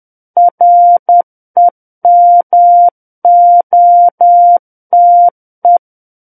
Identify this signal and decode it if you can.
Morse code, heard as a keyed tone, about 10 wpm, 700 Hz. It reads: REMOTE
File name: Book One / 01-Chapter 1 - 10 WPM